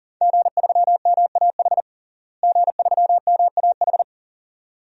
Morse code, heard as a keyed tone, about 37 wpm, 700 Hz. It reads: G3MAH G3MAH
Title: DXCC-100 Call Signs - 5 Characters - Repeated 2x - 40wpm